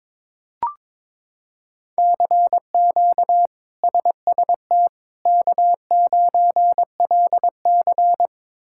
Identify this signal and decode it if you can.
Morse code, heard as a keyed tone, about 22 wpm, 700 Hz. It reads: CQ SST K9LC
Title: K1USN SST QSO Practice - 22wpm